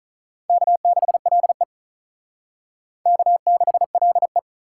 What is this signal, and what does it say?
Morse code, heard as a keyed tone, about 39 wpm, 700 Hz. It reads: K6LE K6LE
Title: Speed Racing - USA 1x2 Call Signs 35wpm